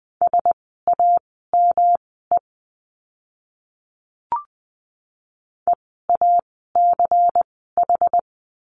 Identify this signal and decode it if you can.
Morse code, heard as a keyed tone, about 20 wpm, spaced as for 14 wpm, 700 Hz. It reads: SAME EACH
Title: 192 Lesson 31f - Letter v - Words 20wpm